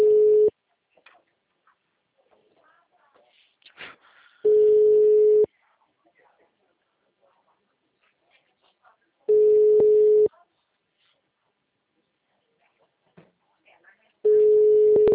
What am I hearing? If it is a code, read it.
Ringback tone (Europe, 425 Hz), with speech.
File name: out- -2009- - - .57387